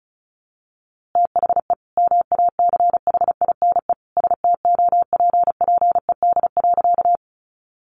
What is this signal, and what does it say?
Morse code, heard as a keyed tone, about 34 wpm, 700 Hz. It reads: THE MACHINE STOPPED.
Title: A Sound Of Thunder 35wpm